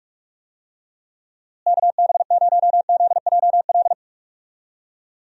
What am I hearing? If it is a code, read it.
Morse code, heard as a keyed tone, about 42 wpm, 700 Hz. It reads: KB0ZJL
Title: USA Call Signs - 6 Characters - Repeated 2x - 45wpm